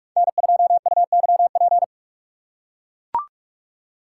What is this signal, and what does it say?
Morse code, heard as a keyed tone, about 43 wpm, 700 Hz. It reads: N2UYP
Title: USA Call Signs - 5 Characters - 3x Character Spacing - 45wpm